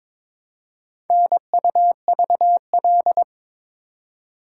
Morse code, heard as a keyed tone, about 22 wpm, 700 Hz. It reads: NUVL
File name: practice / 22wpm / Four Letter-Number Combinations - 22wpm